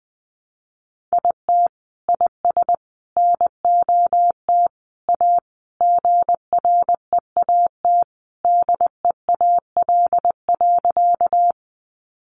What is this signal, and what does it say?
Morse code, heard as a keyed tone, about 20 wpm, 700 Hz. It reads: IT IS NOT A GREAT DEAL.
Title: Sentences from Top 800 Words 20wpm